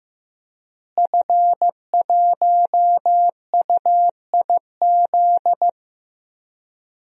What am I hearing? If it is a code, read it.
Morse code, heard as a keyed tone, about 15 wpm, 700 Hz. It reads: F1UIZ
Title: DXCC-100 Call Signs - 5 Characters - Repeated 2x - 15wpm